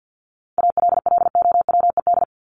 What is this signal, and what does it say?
Morse code, heard as a keyed tone, about 50 wpm, 700 Hz. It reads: ALLOWED
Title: ED Suffix - Top 100 Words with Base Words 50wpm